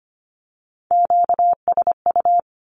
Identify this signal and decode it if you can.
Morse code, heard as a keyed tone, about 25 wpm, 700 Hz. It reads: QSU
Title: Q-Codes with Spoken Meaning - 25wpm